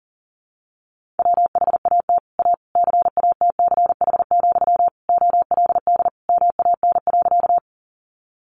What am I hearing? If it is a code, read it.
Morse code, heard as a keyed tone, about 40 wpm, 700 Hz. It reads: WHAT A CATCH, OLD MAN.
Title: Sentences from Top 1500 Words 40wpm